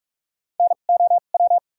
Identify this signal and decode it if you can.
Morse code, heard as a keyed tone, about 42 wpm, spaced as for 32 wpm, 700 Hz. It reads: NOW
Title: 2x Character Spacing - Top 300 Words - 45wpm